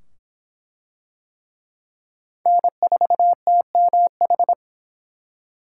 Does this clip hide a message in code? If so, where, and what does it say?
Yes: Morse code from 2.5 to 4.5 s, heard as a keyed tone, about 23 wpm, 700 Hz. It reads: N4TMH